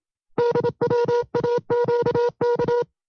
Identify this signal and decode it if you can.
Morse code, heard as a keyed tone, about 27 wpm, 480 Hz. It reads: DWAQK
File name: FreeFiles / CW-practice / 25WPM-side2